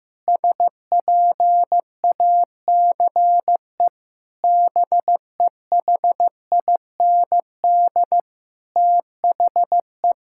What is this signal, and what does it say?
Morse code, heard as a keyed tone, about 16 wpm, 700 Hz. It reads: SPACE BEHIND THE
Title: Sentences from Top 1000 Words - Review Part 5 - 15wpm